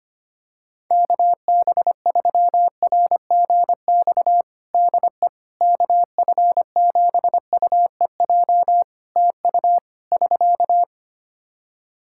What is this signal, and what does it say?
Morse code, heard as a keyed tone, about 25 wpm, 700 Hz. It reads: KB3RGX DE KF7UEJ TU <SK>
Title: Practice QSO - Exchange 4 - Ending - 25wpm